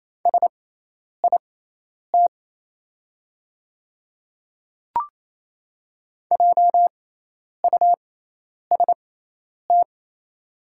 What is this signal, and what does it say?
Morse code, heard as a keyed tone, about 28 wpm, spaced as for 10 wpm, 690 Hz. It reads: SIT JUST
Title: 6x Character Spacing - Top 500 Words - 28wpm